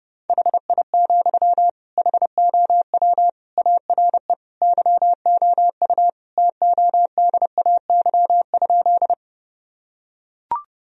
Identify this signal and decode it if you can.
Morse code, heard as a keyed tone, about 30 wpm, 700 Hz. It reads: HI, HOW ARE YOU TODAY?
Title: Sentences from Top 1000 Words - Review Part 3 - 30wpm